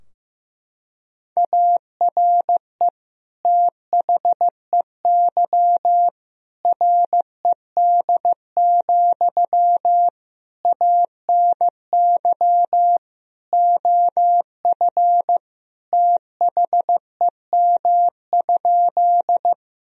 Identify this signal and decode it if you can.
Morse code, heard as a keyed tone, about 15 wpm, 700 Hz. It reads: ARE THEY RED, ANY OF THEM?
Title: Sentences from Top 1000 Words - Review Part 2 - 15wpm